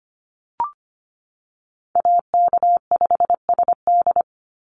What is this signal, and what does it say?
Morse code, heard as a keyed tone, about 25 wpm, 700 Hz. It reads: AK5SD
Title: NA SOTA Chaser Callsigns 25wpm